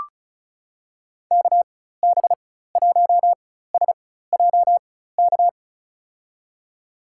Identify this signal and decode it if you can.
Morse code, heard as a keyed tone, about 36 wpm, spaced as for 16 wpm, 700 Hz. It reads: KB1SJK
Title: USA Call Signs - 6 Characters - 4x Character Spacing - 35wpm